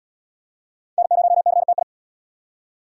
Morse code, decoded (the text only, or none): N0CTI